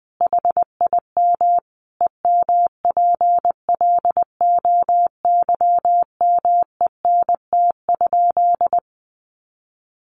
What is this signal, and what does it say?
Morse code, heard as a keyed tone, about 20 wpm, 700 Hz. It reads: HIM EMPLOYMENT?
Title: Sentences from Top 1500 Words 20wpm